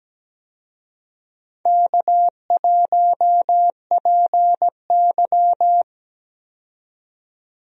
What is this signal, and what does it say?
Morse code, heard as a keyed tone, about 17 wpm, 700 Hz. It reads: K1PY
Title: USA Call Signs - 4 Characters - 17wpm